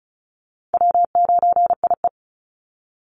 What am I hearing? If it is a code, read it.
Morse code, heard as a keyed tone, about 34 wpm, 700 Hz. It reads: W9IE
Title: Easy Call Signs Repeated 2x - 35wpm